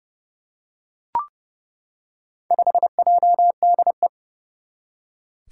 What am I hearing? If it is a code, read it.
Morse code, heard as a keyed tone, about 30 wpm, 700 Hz. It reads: 5JDE
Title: Four Challenging Letter-Number Combinations 30wpm